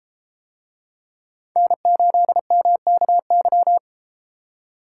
Morse code, heard as a keyed tone, about 33 wpm, 700 Hz. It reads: N8MKY